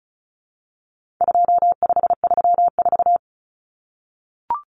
Morse code, heard as a keyed tone, about 34 wpm, 700 Hz. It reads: 2534
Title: Numbers - 4-Digit 35wpm